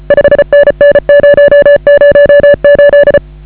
Morse code, heard as a keyed tone, about 34 wpm, 1700 Hz, with noise. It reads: 5NN 008
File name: AUDIO2019 / HA8KAZ CW